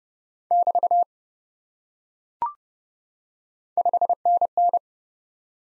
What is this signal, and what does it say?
Morse code, heard as a keyed tone, about 30 wpm, 700 Hz. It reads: <BT> 5NN
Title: Top 90 Abbr 30WPM